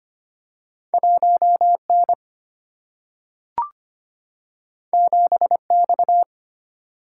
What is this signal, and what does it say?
Morse code, heard as a keyed tone, about 25 wpm, 700 Hz. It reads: 1N 7X
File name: Two Letter-Number Combinations - 25wpm